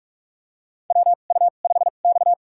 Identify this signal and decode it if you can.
Morse code, heard as a keyed tone, about 46 wpm, spaced as for 33 wpm, 700 Hz. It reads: WU5X
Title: USA Call Signs - 4 Characters - 2x Character Spacing - 45wpm